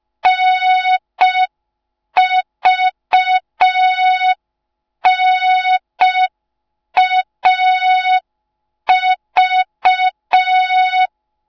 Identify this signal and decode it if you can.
Morse code, heard as a keyed tone, about 5 wpm, 3700 Hz. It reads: NVNAV